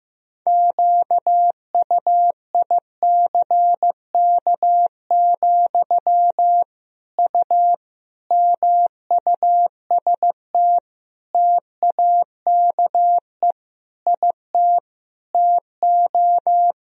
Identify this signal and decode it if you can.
Morse code, heard as a keyed tone, about 16 wpm, 700 Hz. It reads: QUICK, U MUST TAKE IT TO
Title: Sentences from Top 500 Words and Abbreviations 15wpm